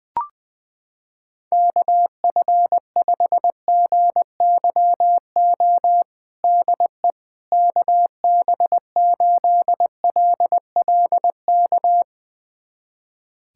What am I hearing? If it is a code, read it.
Morse code, heard as a keyed tone, about 20 wpm, 700 Hz. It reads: KF5GYO DE KB8LLK
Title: USA Callsign DE Callsign 20wpm